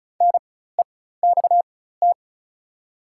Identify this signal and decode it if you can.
Morse code, heard as a keyed tone, about 34 wpm, spaced as for 16 wpm, 700 Hz. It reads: NEXT
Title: 4x Character Spacing - Top 300 Words - 35wpm